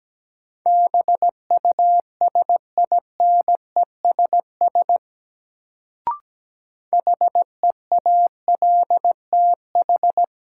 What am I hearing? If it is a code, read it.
Morse code, heard as a keyed tone, about 17 wpm, 700 Hz. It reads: BUSINESS HEALTH
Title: Sentences from Top 600 Words - New Word Drill - 17wpm